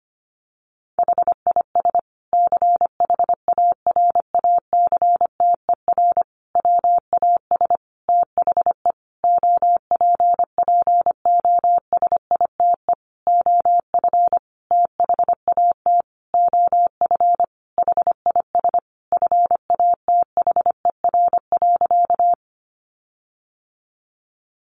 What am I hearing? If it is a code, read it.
Morse code, heard as a keyed tone, about 25 wpm, 700 Hz. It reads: HIS CHARACTER WAS THE OPPOSITE OF THAT OF HIS FATHER.
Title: Sentences from Top 1300 Words 25wpm